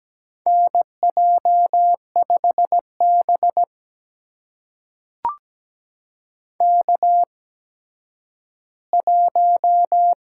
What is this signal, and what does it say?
Morse code, heard as a keyed tone, about 17 wpm, spaced as for 10 wpm, 700 Hz. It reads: NJ5B K1
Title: USA Call Signs - 4 Characters - 8x Character Spacing - 17wpm